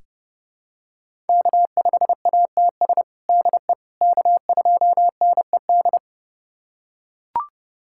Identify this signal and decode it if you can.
Morse code, heard as a keyed tone, about 32 wpm, 700 Hz. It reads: K5ATS DE K2NED
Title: USA Callsign DE Callsign - 5 Characters - 30wpm